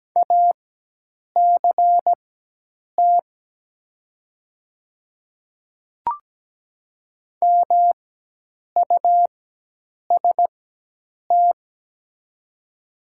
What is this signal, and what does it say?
Morse code, heard as a keyed tone, about 17 wpm, spaced as for 8 wpm, 700 Hz. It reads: ACT MUST
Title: 4x Character Spacing - Top 200 Words - 17wpm